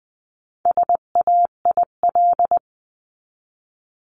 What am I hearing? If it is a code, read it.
Morse code, heard as a keyed tone, about 20 wpm, 700 Hz. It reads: SAIL
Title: 259 Lesson 42d - Words 18-20wpm